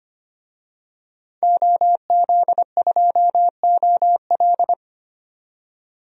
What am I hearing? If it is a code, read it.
Morse code, heard as a keyed tone, about 25 wpm, 700 Hz. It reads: OZ2OL
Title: DXCC-100 Call Signs - 5 Characters - 25wpm